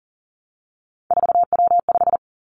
Morse code, heard as a keyed tone, about 38 wpm, 700 Hz. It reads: 4W5
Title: Three Challenging Letter-Number Combinations 40wpm